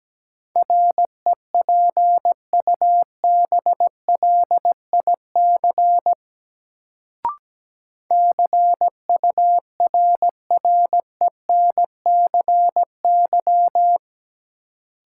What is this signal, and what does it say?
Morse code, heard as a keyed tone, about 17 wpm, 700 Hz. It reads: REPUBLIC CURRENCY
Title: Sentences from Top 1500 Words - New Word Drill Alt - 17wpm